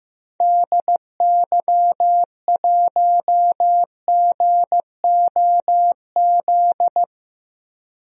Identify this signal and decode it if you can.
Morse code, heard as a keyed tone, about 15 wpm, 700 Hz. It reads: DY1GOZ
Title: DXCC-100 Call Signs - 6 Characters - Repeated 2x - 15wpm